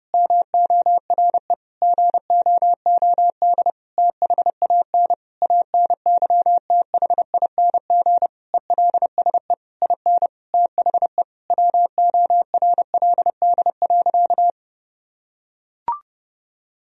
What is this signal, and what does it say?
Morse code, heard as a keyed tone, about 30 wpm, 700 Hz. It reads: MORE GOOD THAN ANYTHING ELSE IN THE WORLD.
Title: Sentences from Top 1000 Words - Review Part 4 - 30wpm